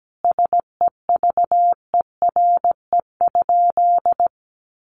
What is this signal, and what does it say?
Morse code, heard as a keyed tone, about 17 wpm, 700 Hz. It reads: SEVERE?
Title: Sentences from Top 1400 Words 17wpm